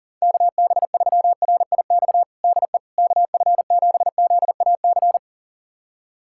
Morse code, heard as a keyed tone, about 37 wpm, 700 Hz. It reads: KB3RIX DE KF7ZAC
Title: USA Callsign DE Callsign - 6 Characters - 40wpm